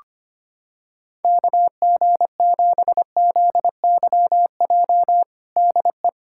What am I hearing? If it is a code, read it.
Morse code, heard as a keyed tone, about 25 wpm, 700 Hz. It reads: KG7ZYJ DE
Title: USA Callsign DE Callsign - 6 Characters - 25wpm